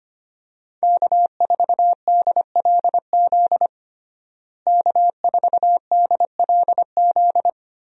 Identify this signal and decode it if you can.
Morse code, heard as a keyed tone, about 25 wpm, 700 Hz. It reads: K4DLZ K4DLZ
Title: USA Call Signs - 5 Characters - Repeated 3x - 25wpm